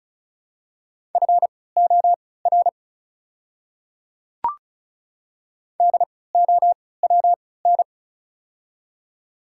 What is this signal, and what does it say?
Morse code, heard as a keyed tone, about 36 wpm, spaced as for 20 wpm, 700 Hz. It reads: FOR DOWN
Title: 3x Character Spacing - Top 300 Words - 35wpm